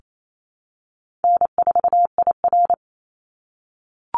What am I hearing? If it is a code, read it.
Morse code, heard as a keyed tone, about 28 wpm, 700 Hz. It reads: N4IR
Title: Easy Call Signs - 28wpm